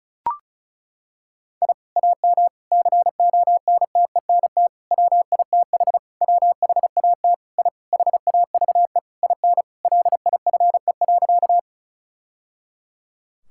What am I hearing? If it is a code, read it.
Morse code, heard as a keyed tone, about 36 wpm, 700 Hz. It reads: I AM CONTENT WITH WHAT I HAVE IN LIFE.
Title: Sentences from Top 1000 Words - Review Part 6 - 35wpm